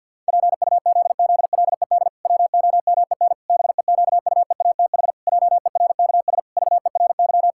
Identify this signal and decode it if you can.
Morse code, heard as a keyed tone, about 50 wpm, 700 Hz. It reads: PUZZLED WOMEN BEQUEATH JERKS VERY
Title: Pangram Challenge v2 - 50wpm